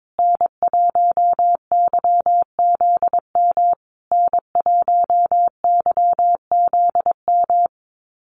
Morse code, heard as a keyed tone, about 22 wpm, 700 Hz. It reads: N1YZM N1YZM
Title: USA Call Signs Repeated 3x 22wpm